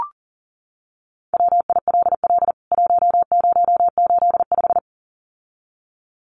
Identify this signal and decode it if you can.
Morse code, heard as a keyed tone, about 40 wpm, 700 Hz. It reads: WILL 1085
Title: CWT Exchange 40wpm